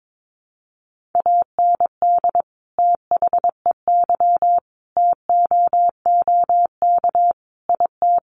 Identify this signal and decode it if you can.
Morse code, heard as a keyed tone, about 22 wpm, 700 Hz. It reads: AND THEY TOOK IT
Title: Sentences from Top 1400 Words 22wpm